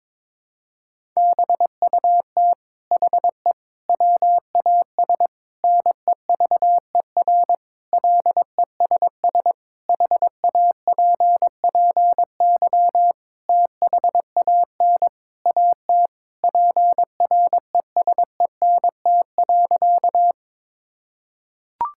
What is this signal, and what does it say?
Morse code, heard as a keyed tone, about 22 wpm, 700 Hz. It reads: BUT HE WAS NEVER LESS HAPPY THAN AT PRESENT.
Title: Sentences from Top 1000 Words - Review Part 5 - 22wpm